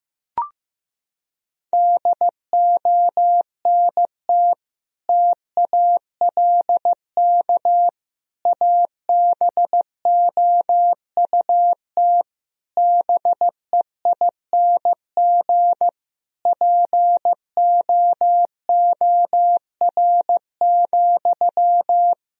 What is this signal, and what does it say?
Morse code, heard as a keyed tone, about 15 wpm, 700 Hz. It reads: DONT TALK ABOUT BEING POOR,
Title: Sentences from Top 1000 Words - Review Part 4 - 15wpm